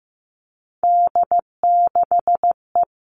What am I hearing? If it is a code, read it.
Morse code, heard as a keyed tone, about 15 wpm, 700 Hz. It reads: D6E
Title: All Cut Numbers - 3-Digit 15wpm